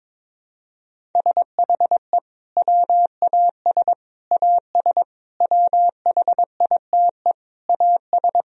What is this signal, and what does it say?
Morse code, heard as a keyed tone, about 22 wpm, 700 Hz. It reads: SHE WAS AS WHITE AS
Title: Sentences from Top 1000 Words - Review Part 1 - 22wpm